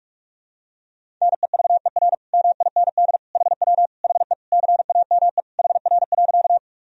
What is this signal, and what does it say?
Morse code, heard as a keyed tone, about 41 wpm, 700 Hz. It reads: NEVER MIND HW HE CAME HR.
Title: Sentences from Top 500 Words and Abbreviations 45wpm